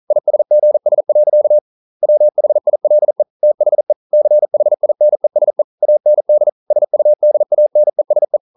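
Morse code, heard as a keyed tone, about 42 wpm, 600 Hz. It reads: ISGS. WHILE THE CHINESE AND SUDANESE
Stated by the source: Morse code practice at 41 wpm, from CW med Gnister.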